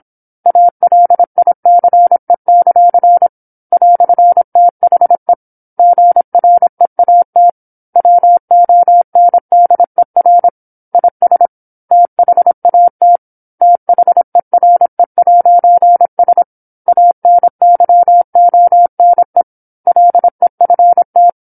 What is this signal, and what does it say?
Morse code, heard as a keyed tone, about 27 wpm, 700 Hz. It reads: ALICE; "THE GREAT WONDER IS THAT THERE'S ANYONE LEFT
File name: 08-Chapter-08 26 WPM